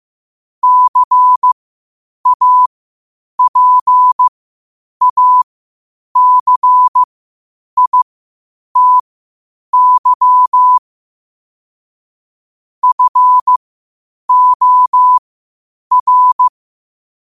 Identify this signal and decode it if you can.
Morse code, heard as a keyed tone, about 15 wpm, spaced as for 8 wpm, 1000 Hz. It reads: CAPACITY FOR
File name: qotdfiles / 8wpm